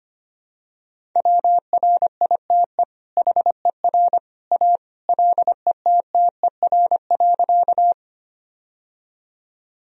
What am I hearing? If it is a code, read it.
Morse code, heard as a keyed tone, about 25 wpm, 700 Hz. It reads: WRITE HER A LETTER.
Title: Sentences from Top 1000 Words - Review Part 1 - 25wpm